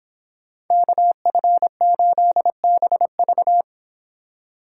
Morse code, heard as a keyed tone, about 26 wpm, 700 Hz. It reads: KF8BV